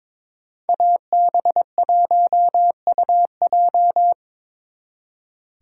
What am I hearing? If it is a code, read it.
Morse code, heard as a keyed tone, about 27 wpm, 700 Hz. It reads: AB1UJ
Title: Speed Racing - USA Call Signs 22wpm